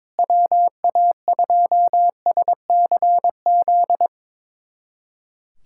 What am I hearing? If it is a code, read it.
Morse code, heard as a keyed tone, about 22 wpm, 700 Hz. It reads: WA2SCZ
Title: USA Callsign DE Callsign - 6 Characters - 22wpm